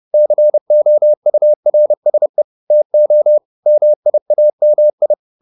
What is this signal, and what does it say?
Morse code, heard as a keyed tone, about 30 wpm, 600 Hz. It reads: COURSE TO MIAMI